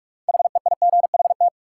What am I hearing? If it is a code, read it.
Morse code, heard as a keyed tone, about 41 wpm, 700 Hz. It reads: HEIGHT
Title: Sentences from Top 1600 Words - New Word Drill - 45wpm